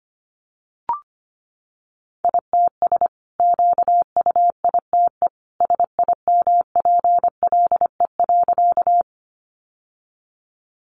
Morse code, heard as a keyed tone, about 25 wpm, 700 Hz. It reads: ITS QUITE SIMPLE.
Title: Sentences from Top 700 Words 25wpm